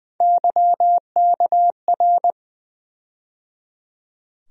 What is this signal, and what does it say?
Morse code, heard as a keyed tone, about 20 wpm, 700 Hz. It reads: YKR
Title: USA Call Sign Postfixes - 3 Letters 20wpm